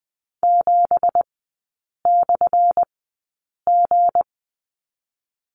Morse code, heard as a keyed tone, about 20 wpm, spaced as for 8 wpm, 700 Hz. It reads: 7/G
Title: 205 Lesson 33e - Letter j - Three Characters 20wpm